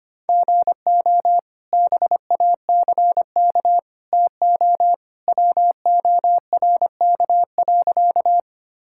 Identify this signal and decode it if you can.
Morse code, heard as a keyed tone, about 25 wpm, 700 Hz. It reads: GO BACK TO WORK.
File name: practice / 25wpm / Sentences with 2-Word Phrases - 25wpm